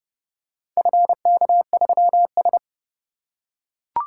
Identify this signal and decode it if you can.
Morse code, heard as a keyed tone, about 29 wpm, 700 Hz. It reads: FK3S